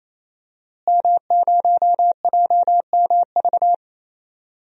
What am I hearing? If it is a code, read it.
Morse code, heard as a keyed tone, about 28 wpm, 700 Hz. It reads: M0JMV
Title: DXCC-100 Call Signs - 5 Characters - Repeated 3x - 28wpm